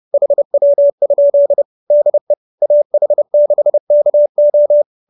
Morse code, heard as a keyed tone, about 30 wpm, 570 Hz. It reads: HW? DE AH6KO
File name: QSO311 30